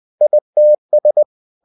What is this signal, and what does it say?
Morse code, heard as a keyed tone, about 20 wpm, 600 Hz. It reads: ITS